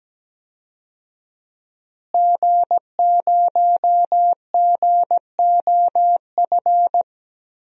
Morse code, heard as a keyed tone, about 17 wpm, 700 Hz. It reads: G0GOF
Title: DXCC-100 Call Signs - 5 Characters - Repeated 2x - 17wpm